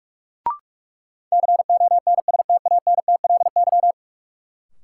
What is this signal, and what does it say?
Morse code, heard as a keyed tone, about 45 wpm, 700 Hz. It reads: CONSTANTLY
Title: Sentences from Top 1000 Words - Review Part 7 - New Word Drill - 45wpm